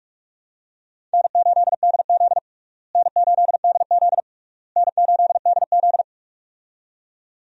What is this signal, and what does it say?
Morse code, heard as a keyed tone, about 46 wpm, 700 Hz. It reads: N8DZ N8DZ N8DZ
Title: USA Call Signs - 4 Characters - Repeated 3x - 45wpm